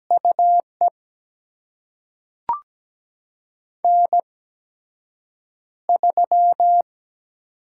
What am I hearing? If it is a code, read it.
Morse code, heard as a keyed tone, about 17 wpm, spaced as for 7 wpm, 700 Hz. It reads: UE N3
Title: USA Call Signs - 4 Characters - 8x Character Spacing - 17wpm